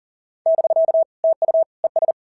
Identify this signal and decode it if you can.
Morse code, heard as a keyed tone, about 40 wpm, 650 Hz. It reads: <BK> TU ES
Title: SOTA QSO Practice 40wpm